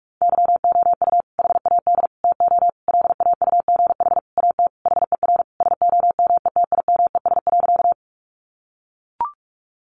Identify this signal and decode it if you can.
Morse code, heard as a keyed tone, about 44 wpm, 700 Hz. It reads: YOU HAD TO LAUGH AT HER SOMETIMES.
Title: Sentences from Top 600 Words 45wpm